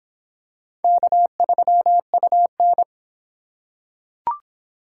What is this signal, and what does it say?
Morse code, heard as a keyed tone, about 24 wpm, 700 Hz. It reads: K3UN